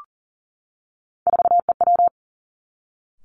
Morse code, heard as a keyed tone, about 38 wpm, 690 Hz. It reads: 4EW
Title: Three Challenging Letter-Number Combinations 40wpm